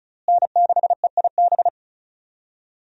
Morse code, heard as a keyed tone, about 34 wpm, 700 Hz. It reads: N6EIB
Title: USA Call Signs - 5 Characters - Repeated 3x - 35wpm